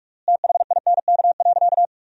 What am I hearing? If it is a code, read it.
Morse code, heard as a keyed tone, about 46 wpm, 700 Hz. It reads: THINK.
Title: 3x Word-Spacing - Sentences from Top 400 Words - 45wpm